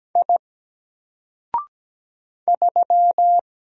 Morse code, heard as a keyed tone, about 17 wpm, 700 Hz. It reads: I 3
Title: Single Letter-Number Alt 17wpm